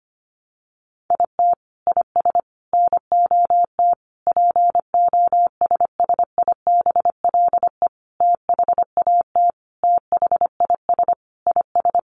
Morse code, heard as a keyed tone, about 25 wpm, 700 Hz. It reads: IT IS NOT POSSIBLE THAT THIS IS
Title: Sentences from Top 1200 Words 25wpm